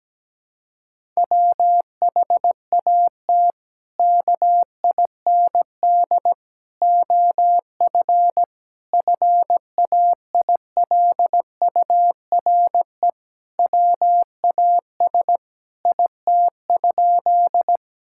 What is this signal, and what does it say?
Morse code, heard as a keyed tone, about 17 wpm, 700 Hz. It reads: WHAT KIND OF FAILURE WAS IT?
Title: Sentences from Top 1000 Words - Review Part 6 - 17wpm